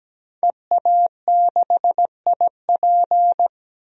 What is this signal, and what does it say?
Morse code, heard as a keyed tone, about 17 wpm, 700 Hz. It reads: EA6IP
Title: DXCC-100 Call Signs - 5 Characters - Repeated 3x - 17wpm